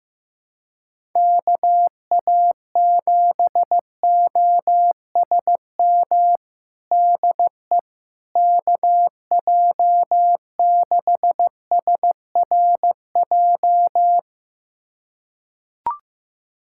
Morse code, heard as a keyed tone, about 15 wpm, 700 Hz. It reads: KA7OSM DE KJ6SRJ
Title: USA Callsign DE Callsign 15wpm